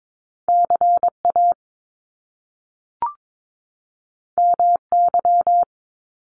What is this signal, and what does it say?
Morse code, heard as a keyed tone, about 22 wpm, 700 Hz. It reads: CA MY
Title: Top Two Letter Combinations in Words 22wpm